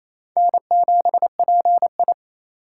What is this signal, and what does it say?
Morse code, heard as a keyed tone, about 35 wpm, 700 Hz. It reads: N7PI
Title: Speed Racing - USA 1x2 Call Signs 28wpm